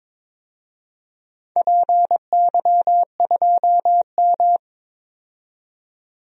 Morse code, heard as a keyed tone, about 21 wpm, 700 Hz. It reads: PY2M
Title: DXCC-100 Call Signs - 4 Characters - 22wpm